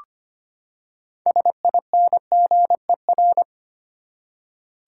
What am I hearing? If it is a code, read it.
Morse code, heard as a keyed tone, about 25 wpm, 700 Hz. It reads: SINGER
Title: Six Letter Words - 25wpm